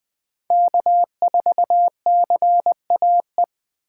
Morse code, heard as a keyed tone, about 20 wpm, 700 Hz. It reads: K4CAE
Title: 2023 POTA Hunter Call Signs - 20wpm